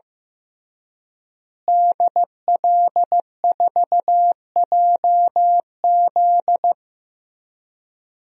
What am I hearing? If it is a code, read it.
Morse code, heard as a keyed tone, about 15 wpm, 700 Hz. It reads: DL4JZ
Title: DXCC-100 Call Signs - 5 Characters - Repeated 3x - 15wpm